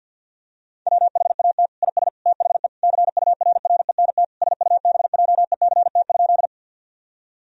Morse code, heard as a keyed tone, about 50 wpm, 700 Hz. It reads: WHAT IS THE CURRENT SUBJECT?
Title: Sentences from Top 1000 Words - Review Part 4 - 50wpm